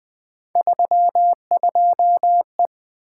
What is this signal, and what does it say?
Morse code, heard as a keyed tone, about 20 wpm, 700 Hz. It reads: 32E
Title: Three Letter-Number Combinations - 20wpm